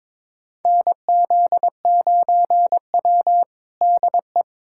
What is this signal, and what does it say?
Morse code, heard as a keyed tone, about 21 wpm, 700 Hz. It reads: NZ9W DE
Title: USA Callsign DE Callsign - 4 Characters - 22wpm